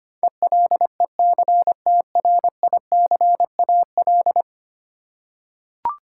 Morse code, heard as a keyed tone, about 25 wpm, 700 Hz. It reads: ELECTRICAL
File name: 10 Letter Words - First 3 Letters Given - 25wpm